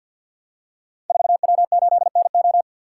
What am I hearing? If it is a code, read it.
Morse code, heard as a keyed tone, about 50 wpm, 700 Hz. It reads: 4W8NO
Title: Code Groups - 5 Letter:Numbers - 50wpm